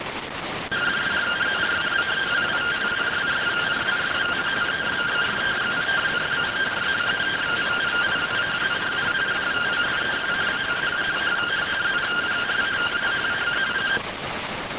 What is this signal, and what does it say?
Two-tone teleprinter signal, about 170 Hz shift.